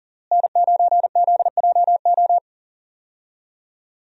Morse code, heard as a keyed tone, about 38 wpm, 700 Hz. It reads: N9ZJO
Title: USA Call Signs - 5 Characters - Repeated 3x - 40wpm